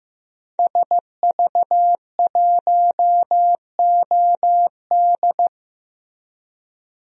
Morse code, heard as a keyed tone, about 15 wpm, 700 Hz. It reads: SV1OD
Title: DXCC-100 Call Signs - 5 Characters - 15wpm